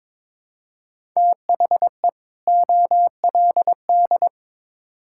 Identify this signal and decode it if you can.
Morse code, heard as a keyed tone, about 22 wpm, 700 Hz. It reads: THE OLD
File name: Build A Phrase - 2-Word Phrases - 22wpm